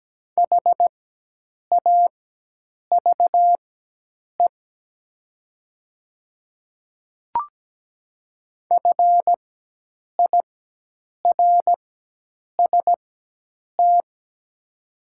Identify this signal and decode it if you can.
Morse code, heard as a keyed tone, about 17 wpm, spaced as for 8 wpm, 700 Hz. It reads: HAVE FIRST
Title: 4x Character Spacing - Top 200 Words - 17wpm